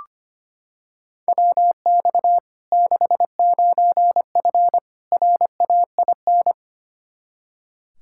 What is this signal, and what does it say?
Morse code, heard as a keyed tone, about 25 wpm, 700 Hz. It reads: WX 69F RAIN
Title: Practice QSO - Exchange 2 - Info - 25wpm